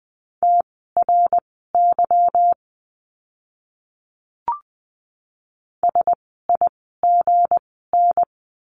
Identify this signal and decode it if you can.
Morse code, heard as a keyed tone, about 20 wpm, spaced as for 14 wpm, 700 Hz. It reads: TRY SIGN
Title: 160 Lesson 26f - Letter g - Words 20wpm